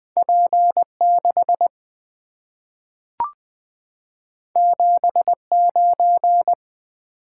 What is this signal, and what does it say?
Morse code, heard as a keyed tone, about 20 wpm, 700 Hz. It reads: P6 79
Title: Two Letter-Number Combinations - 20wpm